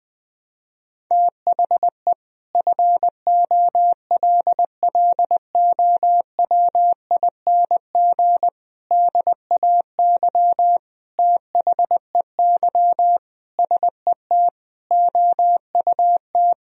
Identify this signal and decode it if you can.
Morse code, heard as a keyed tone, about 20 wpm, 700 Hz. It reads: THE FOLLOWING DAY THEY SET OUT
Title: Sentences from Top 1000 Words - Review Part 4 - 20wpm